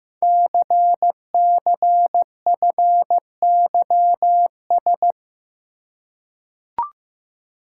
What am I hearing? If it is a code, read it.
Morse code, heard as a keyed tone, about 15 wpm, 700 Hz. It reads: CCFYS
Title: Code Groups - 5 Letters - 15wpm